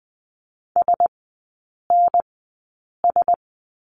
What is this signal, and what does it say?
Morse code, heard as a keyed tone, about 20 wpm, spaced as for 8 wpm, 700 Hz. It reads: SNS
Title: 041 Lesson 8e - Number 1 - Three Characters 20wpm